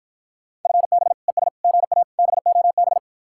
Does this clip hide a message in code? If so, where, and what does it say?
Yes: Morse code from 0.7 to 3.0 s, heard as a keyed tone, about 52 wpm, 700 Hz. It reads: FB ES GA BOB